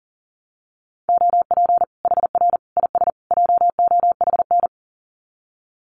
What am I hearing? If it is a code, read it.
Morse code, heard as a keyed tone, about 40 wpm, 700 Hz. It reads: OP HR IS JOHN
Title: Common QSO Phrases 40wpm